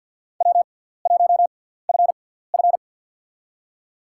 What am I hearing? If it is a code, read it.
Morse code, heard as a keyed tone, about 47 wpm, spaced as for 17 wpm, 700 Hz. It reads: W1FF
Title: USA Call Signs - 6x Character Spacing - 50wpm